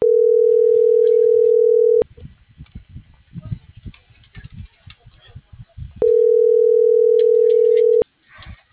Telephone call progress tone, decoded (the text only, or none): ringback tone